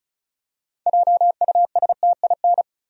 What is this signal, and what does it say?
Morse code, heard as a keyed tone, about 34 wpm, 700 Hz. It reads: JUSTIN
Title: USA Common Names - 35wpm